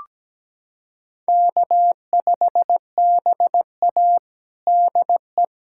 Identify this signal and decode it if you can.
Morse code, heard as a keyed tone, about 17 wpm, 700 Hz. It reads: K5BA DE
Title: USA Callsign DE Callsign - 4 Characters - 17wpm